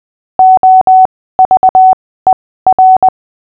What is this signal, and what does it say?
Morse code, heard as a keyed tone, about 20 wpm, spaced as for 15 wpm, 750 Hz. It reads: OVER
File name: cw / fists / Short-Story-6-15wpm